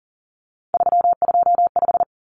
Morse code, heard as a keyed tone, about 40 wpm, 700 Hz. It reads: 325
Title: Numbers - 3-Digit 40wpm